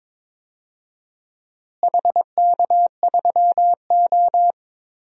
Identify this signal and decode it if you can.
Morse code, heard as a keyed tone, about 22 wpm, 700 Hz. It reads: HK3O
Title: DXCC-100 Call Signs - 4 Characters - Repeated 2x - 22wpm